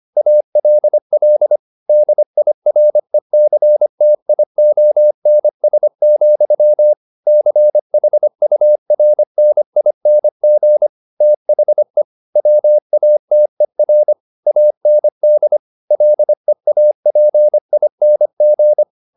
Morse code, heard as a keyed tone, about 25 wpm, 600 Hz. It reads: ALL DIRECTIONS, CHURNING THE WATER AND LEAPING
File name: the old man and the sea part 1 25wpm